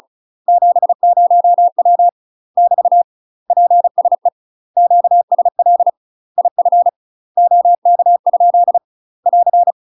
Morse code, heard as a keyed tone, about 36 wpm, 700 Hz. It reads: 70W <BT> PSE QSL IF OK? <AR>
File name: UK MASTER QSOs TRAINING / 35WPM / UK MASTER QSOs 0013